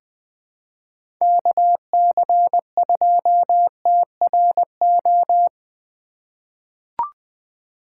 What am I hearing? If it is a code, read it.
Morse code, heard as a keyed tone, about 20 wpm, 700 Hz. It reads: KC2TRO
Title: USA Call Signs - 6 Characters - 3x Character Spacing - 20wpm